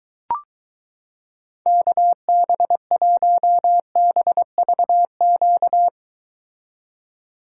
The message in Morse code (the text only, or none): KB1BVQ